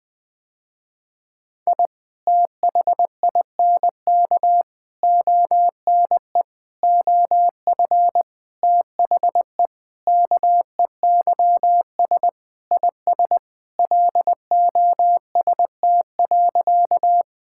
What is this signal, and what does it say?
Morse code, heard as a keyed tone, about 21 wpm, 700 Hz. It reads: I THINK ONE OF THE KEYS IS LOST.
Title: Sentences with 5-Word Phrases - Spoken Beginning - 20wpm